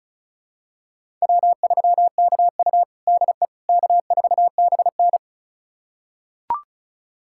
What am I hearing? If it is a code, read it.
Morse code, heard as a keyed tone, about 36 wpm, 700 Hz. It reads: W3KU DE K4BN
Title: USA Callsign DE Callsign - 4 Characters - 35wpm